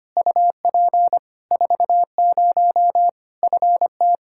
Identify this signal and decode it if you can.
Morse code, heard as a keyed tone, about 25 wpm, 700 Hz. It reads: UP 40 FT
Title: Practice QSO - Exchange 2 - Info - 25wpm